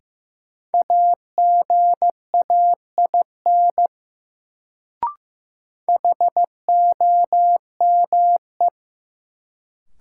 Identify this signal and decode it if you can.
Morse code, heard as a keyed tone, about 15 wpm, 700 Hz. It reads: AGAIN HOME
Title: Sentences from Top 200 Words - New Word Drill - 15wpm